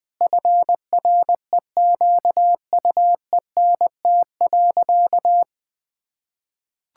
Morse code, heard as a keyed tone, about 20 wpm, 700 Hz. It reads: FREQUENT.
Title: Sentences with 6-Word Phrases - 20wpm